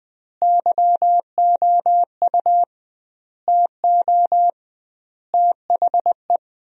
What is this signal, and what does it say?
Morse code, heard as a keyed tone, about 21 wpm, 700 Hz. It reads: YOU TO THE
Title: 2x Word-Spacing - Sentences from Top 400 Words - 20wpm